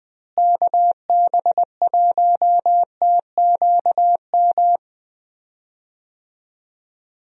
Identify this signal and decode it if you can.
Morse code, heard as a keyed tone, about 20 wpm, 700 Hz. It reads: KB1TQM
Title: USA Call Signs - 6 Characters - Repeated 3x - 20wpm